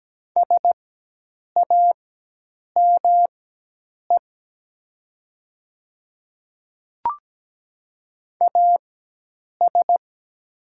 Morse code, heard as a keyed tone, about 17 wpm, spaced as for 8 wpm, 700 Hz. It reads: SAME AS
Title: 4x Character Spacing - Top 300 Words - 17wpm